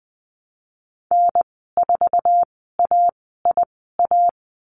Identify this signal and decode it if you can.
Morse code, heard as a keyed tone, about 20 wpm, spaced as for 14 wpm, 700 Hz. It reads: N4AIA
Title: 048 Lesson 9g - Number 4 - Call Signs 20wpm